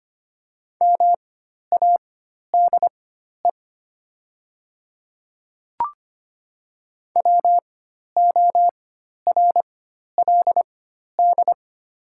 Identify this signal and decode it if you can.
Morse code, heard as a keyed tone, about 25 wpm, spaced as for 12 wpm, 700 Hz. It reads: MADE WORLD
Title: 4x Character Spacing - Top 200 Words - 25wpm